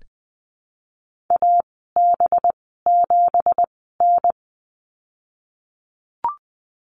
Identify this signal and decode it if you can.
Morse code, heard as a keyed tone, about 21 wpm, spaced as for 15 wpm, 700 Hz. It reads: AB7N
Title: 186 Lesson 30g - Letter b - Call Signs 20wpm